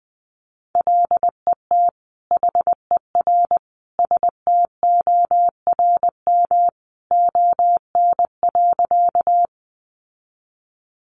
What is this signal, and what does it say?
Morse code, heard as a keyed tone, about 20 wpm, 700 Hz. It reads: LET HER STORM ON.
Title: Sentences from Top 1500 Words 20wpm